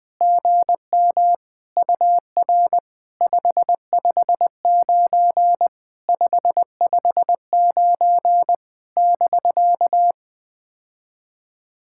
Morse code, heard as a keyed tone, about 20 wpm, 700 Hz. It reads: GM UR 559 559 <BK>
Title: POTA QSO Practice - 20wpm